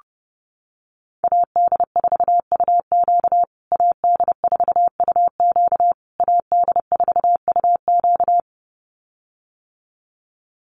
Morse code, heard as a keyed tone, about 30 wpm, 700 Hz. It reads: AD4UQ AD4UQ AD4UQ
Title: USA Call Signs Repeated 3x 30wpm